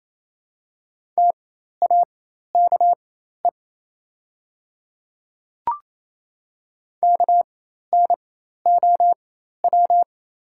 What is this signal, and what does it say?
Morse code, heard as a keyed tone, about 27 wpm, spaced as for 13 wpm, 700 Hz. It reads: TAKE KNOW
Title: 4x Character Spacing - Top 300 Words - 28wpm